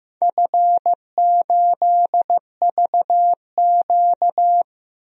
Morse code, heard as a keyed tone, about 15 wpm, 700 Hz. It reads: F8VQ
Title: DXCC-100 Call Signs - 4 Characters - 15wpm